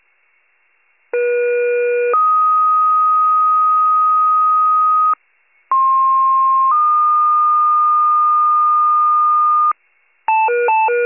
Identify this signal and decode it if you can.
Two-tone page, heard as a one-second tone, then a three-second tone, about 480 then 1250 Hz, about 1060 then 1220 Hz. It is heard from 1.1 to 9.7 s.